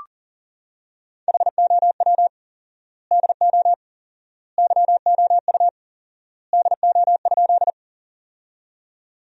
Morse code, heard as a keyed tone, about 40 wpm, 700 Hz. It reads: HOW DO YOU DO?
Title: 4x Word-Spacing - Sentences from Top 100 Words - 40wpm